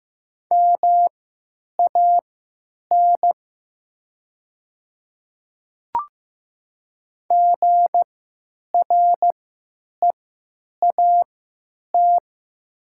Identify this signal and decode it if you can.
Morse code, heard as a keyed tone, about 15 wpm, spaced as for 8 wpm, 700 Hz. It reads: MAN GREAT
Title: 3x Character Spacing - Top 200 Words - 15wpm